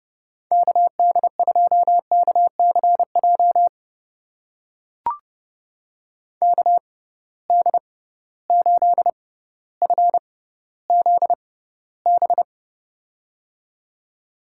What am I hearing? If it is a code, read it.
Morse code, heard as a keyed tone, about 30 wpm, spaced as for 15 wpm, 700 Hz. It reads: KD2KCJ KD8FZB
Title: USA Call Signs - 6 Characters - 6x Character Spacing - 30wpm